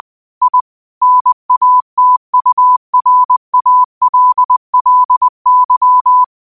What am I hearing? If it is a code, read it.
Morse code, heard as a keyed tone, about 20 wpm, 1000 Hz. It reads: I NATURALLY
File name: qotdfiles / 20wpm